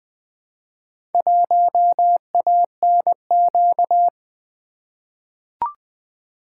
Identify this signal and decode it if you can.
Morse code, heard as a keyed tone, about 20 wpm, 700 Hz. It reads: 1ANQ
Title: Four Letter-Number Combinations - 20wpm